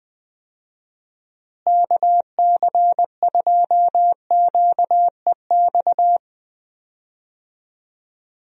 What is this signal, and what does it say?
Morse code, heard as a keyed tone, about 20 wpm, 700 Hz. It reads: KC2QEX
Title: USA Call Signs - 6 Characters - Repeated 2x - 20wpm